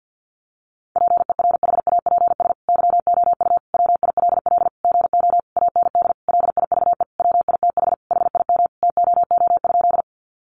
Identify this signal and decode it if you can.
Morse code, heard as a keyed tone, about 50 wpm, 700 Hz. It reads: PERHAPS YOU WILL GO AND LIVE WITH HIM TOO?
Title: Sentences from Top 500 Words 50wpm